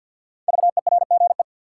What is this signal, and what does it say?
Morse code, heard as a keyed tone, about 51 wpm, 700 Hz. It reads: VERGE
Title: Sets of 4 Words - New Word Drill - 50wpm